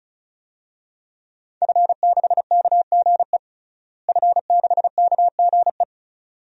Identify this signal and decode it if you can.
Morse code, heard as a keyed tone, about 36 wpm, 700 Hz. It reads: F6KGE F6KGE
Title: DXCC-100 Call Signs - 5 Characters - Repeated 2x - 35wpm